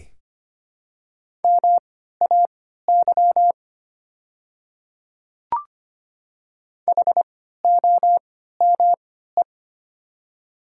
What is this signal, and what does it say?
Morse code, heard as a keyed tone, about 25 wpm, spaced as for 14 wpm, 700 Hz. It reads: MAY HOME
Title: 3x Character Spacing - Top 200 Words - 25wpm